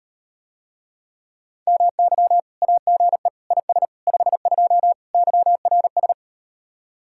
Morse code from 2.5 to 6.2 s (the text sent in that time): AGE IS 52 YRS